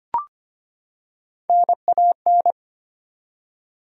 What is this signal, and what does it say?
Morse code, heard as a keyed tone, about 25 wpm, 700 Hz. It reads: NAN
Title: USA Common Names - Max 4 Letters - 25wpm